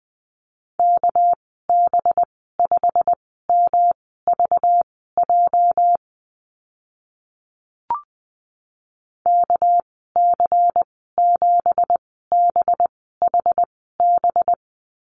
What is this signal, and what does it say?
Morse code, heard as a keyed tone, about 20 wpm, spaced as for 14 wpm, 700 Hz. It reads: KB5MVJ KC7BHB
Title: 219 Lesson 35g - Number 0 - Call Signs 20wpm